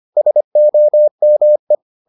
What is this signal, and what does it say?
Morse code, heard as a keyed tone, about 25 wpm, 600 Hz. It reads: SOME